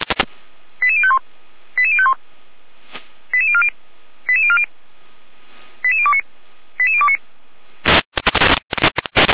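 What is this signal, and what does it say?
Five-tone selective call (ZVEI), heard as quick runs of five tones, tones of about 70 ms. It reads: 88051 88051 88040 88040 88029 88029
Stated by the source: ZVEI five-tone selective call received on an SDR.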